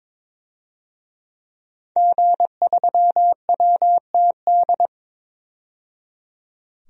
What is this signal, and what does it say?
Morse code, heard as a keyed tone, about 22 wpm, 700 Hz. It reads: G3WTD
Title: DXCC-100 Call Signs - 5 Characters - 22wpm